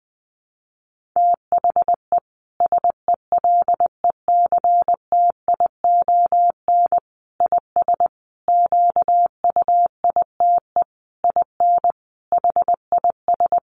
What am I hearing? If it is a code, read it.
Morse code, heard as a keyed tone, about 20 wpm, 700 Hz. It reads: THE SELECTION IS QUITE IN HIS
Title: Sentences from Top 1400 Words 20wpm